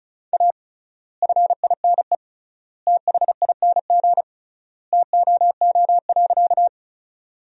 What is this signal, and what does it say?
Morse code, heard as a keyed tone, about 34 wpm, 700 Hz. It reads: A FINE THING TOO.
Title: 3x Word-Spacing - Sentences from Top 500 Words - 35wpm